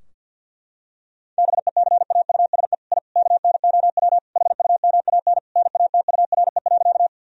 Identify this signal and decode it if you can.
Morse code, heard as a keyed tone, about 52 wpm, 700 Hz. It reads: BECAUSE I KNOW HUMAN NATURE.
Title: Sentences from Top 1000 Words - Review Part 3 - 50wpm